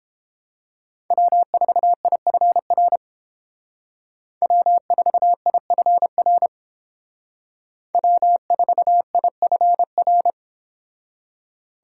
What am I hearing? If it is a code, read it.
Morse code, heard as a keyed tone, about 31 wpm, 700 Hz. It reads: W4IFR W4IFR W4IFR